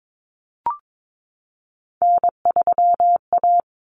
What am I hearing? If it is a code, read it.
Morse code, heard as a keyed tone, about 22 wpm, 700 Hz. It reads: N3A
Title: Common Cut Numbers - 3-Digit 22wpm